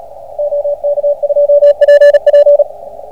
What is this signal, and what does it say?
Morse code, heard as a keyed tone, about 29 wpm, 620 Hz, with radio noise. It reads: OK2PP